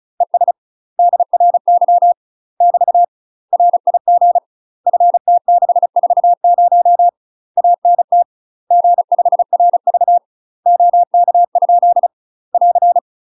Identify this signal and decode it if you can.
Morse code, heard as a keyed tone, about 36 wpm, 700 Hz. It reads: ES DRY <BT> RIG FT640 ANT G5RV OK? <AR>
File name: UK MASTER QSOs TRAINING / 35WPM / UK MASTER QSOs 0036